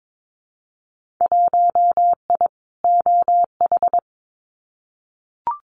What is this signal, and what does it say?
Morse code, heard as a keyed tone, about 22 wpm, 700 Hz. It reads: 1I OH
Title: Winter Field Day Exchanges 22wpm